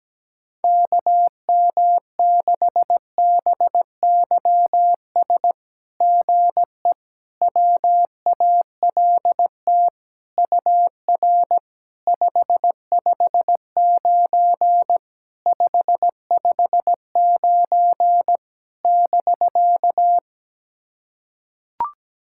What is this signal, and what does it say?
Morse code, heard as a keyed tone, about 17 wpm, 700 Hz. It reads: KM6BYS GE WALT UR 559 559 <BK>
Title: SOTA QSO Practice 17wpm